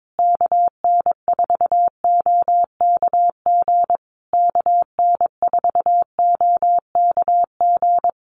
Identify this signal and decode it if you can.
Morse code, heard as a keyed tone, about 22 wpm, 700 Hz. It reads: KN4OKG KN4OKG
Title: USA Call Signs Repeated 3x 22wpm